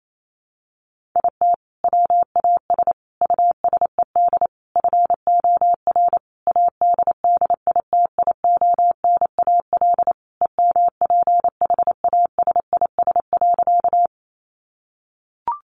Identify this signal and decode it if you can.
Morse code, heard as a keyed tone, about 28 wpm, 700 Hz. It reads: IT WAS USED FOR ADDITIONAL EMPHASIS.
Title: Sentences from Top 1400 Words 28wpm